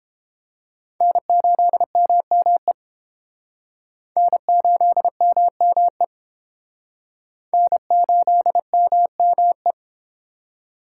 Morse code, heard as a keyed tone, about 31 wpm, 700 Hz. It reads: N8MME N8MME N8MME